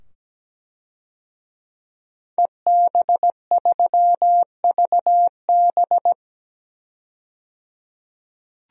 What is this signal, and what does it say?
Morse code, heard as a keyed tone, about 17 wpm, 700 Hz. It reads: EB3VB
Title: DXCC-100 Call Signs - 5 Characters - 17wpm